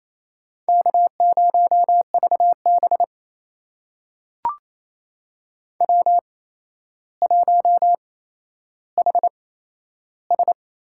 Morse code, heard as a keyed tone, about 28 wpm, spaced as for 12 wpm, 700 Hz. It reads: K0VB W1HS
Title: USA Call Signs - 4 Characters - 8x Character Spacing - 28wpm